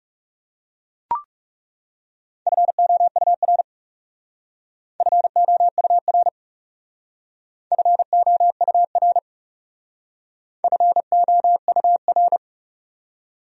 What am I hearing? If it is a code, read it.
Morse code, heard as a keyed tone, about 36 wpm, 700 Hz. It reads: FOUR FOUR FOUR FOUR
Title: Speed-Racing Top 500 Words 30wpm